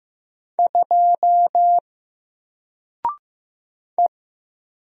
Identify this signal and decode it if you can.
Morse code, heard as a keyed tone, about 15 wpm, 700 Hz. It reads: 2 E
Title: Single Letter-Number 15wpm